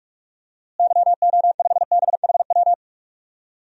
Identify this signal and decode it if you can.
Morse code, heard as a keyed tone, about 45 wpm, 700 Hz. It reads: YO5BHW
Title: DXCC-100 Call Signs - 6 Characters - Repeated 2x - 45wpm